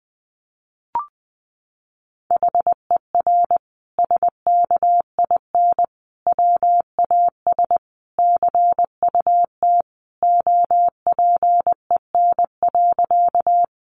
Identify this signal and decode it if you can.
Morse code, heard as a keyed tone, about 19 wpm, 700 Hz. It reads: HER SKIN WAS CUT OPEN.
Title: Sentences from Top 1300 Words 20wpm